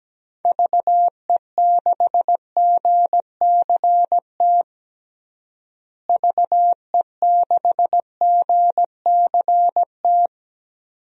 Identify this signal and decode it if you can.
Morse code, heard as a keyed tone, about 17 wpm, 700 Hz. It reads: VE6GCT VE6GCT
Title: DXCC-100 Call Signs - 6 Characters - Repeated 2x - 17wpm